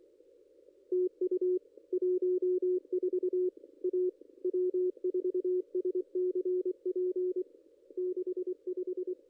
Morse code, heard as a keyed tone, about 24 wpm, 360 Hz. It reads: TU 14 A W4SCP 65
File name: contests / ss-cw-k5tr-op-n5rz -2200